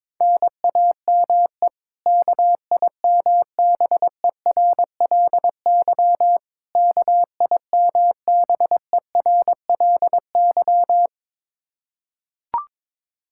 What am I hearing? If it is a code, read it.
Morse code, heard as a keyed tone, about 22 wpm, 700 Hz. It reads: NAME KIMBERLY KIMBERLY
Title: Practice QSO - Exchange 1 - Beginning - 22wpm